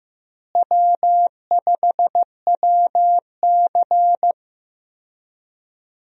Morse code, heard as a keyed tone, about 16 wpm, 700 Hz. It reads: W5WC
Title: USA Call Signs - 4 Characters - Repeated 3x - 15wpm